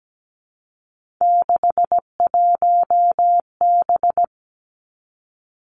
Morse code, heard as a keyed tone, about 17 wpm, 700 Hz. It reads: 61B
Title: Three Challenging Letter-Number Combinations 17wpm